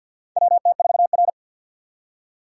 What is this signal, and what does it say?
Morse code, heard as a keyed tone, about 52 wpm, 700 Hz. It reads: WT4R
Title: USA Call Signs - 4 Characters - 3x Character Spacing - 50wpm